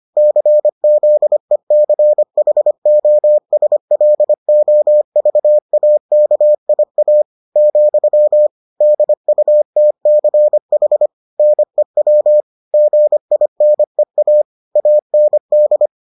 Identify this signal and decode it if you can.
Morse code, heard as a keyed tone, about 25 wpm, 600 Hz. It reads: CZECHOSLOVAKIA , DUTCH NEW GINEA AND